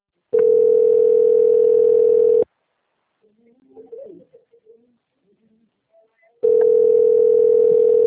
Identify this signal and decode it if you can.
Ringback tone (North America), 440 + 480 Hz.